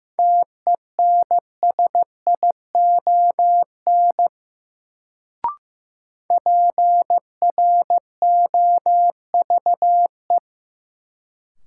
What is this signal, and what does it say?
Morse code, heard as a keyed tone, about 15 wpm, 700 Hz. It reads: TENSION PROVE
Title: Sentences from Top 1600 Words - New Word Drill - 15wpm